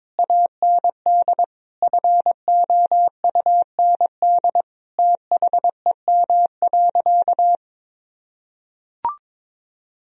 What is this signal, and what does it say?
Morse code, heard as a keyed tone, about 22 wpm, 700 Hz. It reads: AND FOUND THEM.
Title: Sentences from Top 1000 Words - Review Part 1 - 22wpm